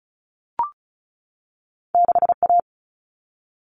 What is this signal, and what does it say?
Morse code, heard as a keyed tone, about 36 wpm, 700 Hz. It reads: 6A
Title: All Cut Numbers - 2-Digit 35wpm